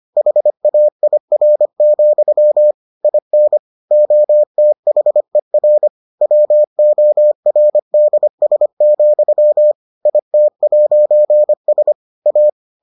Morse code, heard as a keyed tone, about 25 wpm, 600 Hz. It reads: HAIR, IN OTHER WORDS, IT'S A